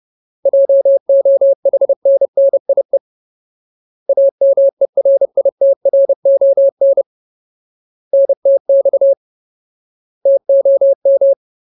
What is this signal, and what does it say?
Morse code, heard as a keyed tone, about 30 wpm, 550 Hz. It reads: JOHNNIE AMERITRON NTX TOM
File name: Ultimate Word List / 30WPM Ultimate